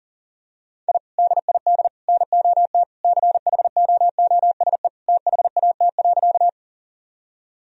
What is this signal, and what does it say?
Morse code, heard as a keyed tone, about 40 wpm, 700 Hz. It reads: I DID NOT CHOOSE THAT.
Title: Sentences from Top 1000 Words - Review Part 4 - 40wpm